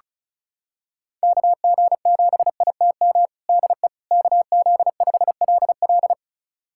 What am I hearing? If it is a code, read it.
Morse code, heard as a keyed tone, about 34 wpm, 700 Hz. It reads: KG7ITM DE KZ5LL
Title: USA Callsign DE Callsign 35wpm